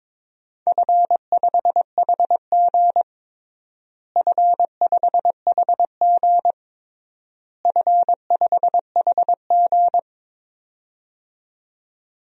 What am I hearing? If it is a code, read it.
Morse code, heard as a keyed tone, about 22 wpm, 700 Hz. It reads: F5HG F5HG F5HG
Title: DXCC-100 Call Signs - 4 Characters - Repeated 3x - 22wpm